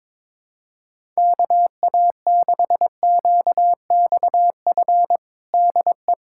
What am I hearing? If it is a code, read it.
Morse code, heard as a keyed tone, about 22 wpm, 700 Hz. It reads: KA6QXF DE
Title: USA Callsign DE Callsign - 6 Characters - 22wpm